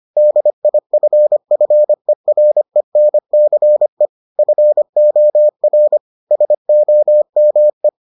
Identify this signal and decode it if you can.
Morse code, heard as a keyed tone, about 25 wpm, 600 Hz. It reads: DIFFERENCE FOR SOME